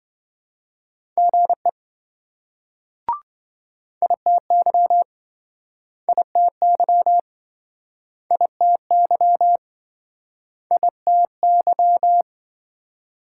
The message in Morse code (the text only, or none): GE ITY ITY ITY ITY